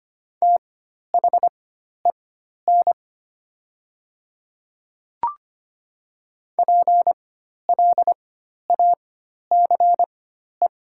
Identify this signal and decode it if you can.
Morse code, heard as a keyed tone, about 25 wpm, spaced as for 12 wpm, 690 Hz. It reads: THEN PLACE
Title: 4x Character Spacing - Top 200 Words - 25wpm